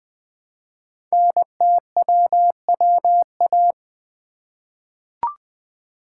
Morse code, heard as a keyed tone, about 20 wpm, 700 Hz. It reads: NTWWA
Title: Code Groups - 5 Letters - 20wpm